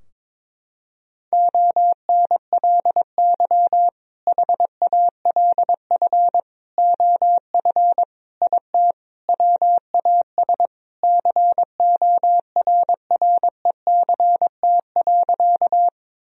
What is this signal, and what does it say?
Morse code, heard as a keyed tone, about 22 wpm, 700 Hz. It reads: ONLY HALF OF IT WAS CORRECT.
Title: Sentences from Top 1000 Words - Review Part 2 - 22wpm